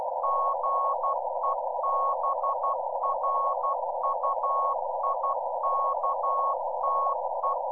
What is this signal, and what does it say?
Morse code, heard as a keyed tone, about 12 wpm, 1100 Hz. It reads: GEBRUIKTE
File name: Z PI4VRZ / PI4VRZ 14-jun-2025